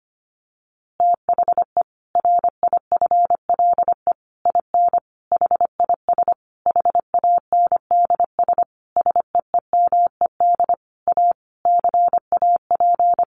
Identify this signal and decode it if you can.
Morse code, heard as a keyed tone, about 25 wpm, 700 Hz. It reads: THE RIFLE IN HIS HANDS SEEMED A CAP
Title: A Sound Of Thunder 25wpm